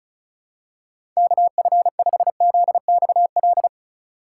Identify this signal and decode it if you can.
Morse code, heard as a keyed tone, about 36 wpm, 700 Hz. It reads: KF5ZXL
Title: USA Call Signs - 6 Characters - 4x Character Spacing - 35wpm